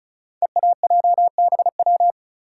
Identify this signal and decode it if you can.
Morse code, heard as a keyed tone, about 36 wpm, 690 Hz. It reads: EAJBW
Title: Code Groups - 5 Letters - 35wpm